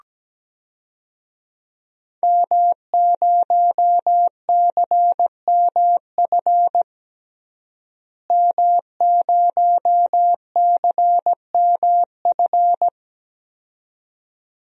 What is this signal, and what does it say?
Morse code, heard as a keyed tone, about 17 wpm, 700 Hz. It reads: M0CMF M0CMF
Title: DXCC-100 Call Signs - 5 Characters - Repeated 2x - 17wpm